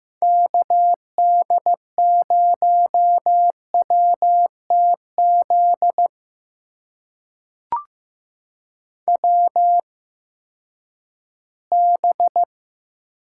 Morse code, heard as a keyed tone, about 15 wpm, spaced as for 10 wpm, 700 Hz. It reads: KD0WTZ WB
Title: USA Call Signs - 6 Characters - 8x Character Spacing - 15wpm